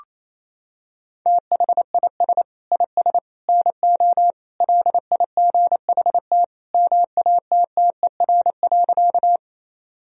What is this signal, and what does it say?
Morse code, heard as a keyed tone, about 28 wpm, 700 Hz. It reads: THIS IS NO LIGHT MATTER.
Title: Sentences from Top 1000 Words - Review Part 1 - 28wpm